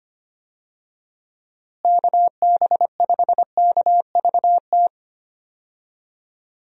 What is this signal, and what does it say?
Morse code, heard as a keyed tone, about 25 wpm, 700 Hz. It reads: KB5KVT
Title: USA Call Signs - 6 Characters - Repeated 3x - 25wpm